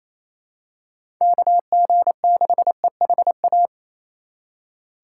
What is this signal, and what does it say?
Morse code, heard as a keyed tone, about 27 wpm, 700 Hz. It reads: KG6EHA
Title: USA Call Signs - 6 Characters - 3x Character Spacing - 28wpm